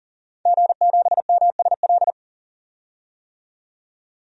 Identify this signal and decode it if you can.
Morse code, heard as a keyed tone, about 40 wpm, 700 Hz. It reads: G7MSL
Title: DXCC-100 Call Signs - 5 Characters - 40wpm